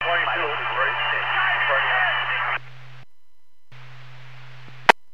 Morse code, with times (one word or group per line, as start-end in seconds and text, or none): none